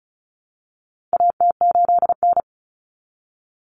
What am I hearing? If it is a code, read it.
Morse code, heard as a keyed tone, about 34 wpm, 700 Hz. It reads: AT8N
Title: Common Cut Numbers - 4-Digit 35wpm